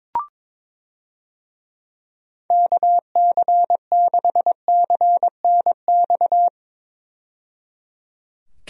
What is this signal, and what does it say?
Morse code, heard as a keyed tone, about 22 wpm, 700 Hz. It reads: KC6CNX
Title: USA Call Signs - 6 Characters - 22wpm